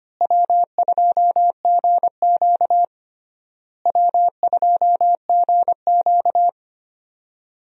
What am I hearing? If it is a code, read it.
Morse code, heard as a keyed tone, about 25 wpm, 700 Hz. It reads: W2GQ W2GQ
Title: USA Call Signs - 4 Characters - Repeated 3x - 25wpm